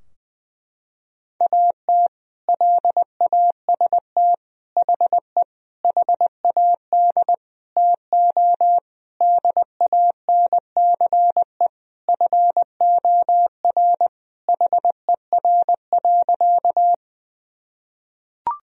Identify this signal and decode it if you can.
Morse code, heard as a keyed tone, about 20 wpm, 700 Hz. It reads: AT LAST HE HAD TO DANCE FOR HER.
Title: Sentences from Top 1000 Words - Review Part 4 - 20wpm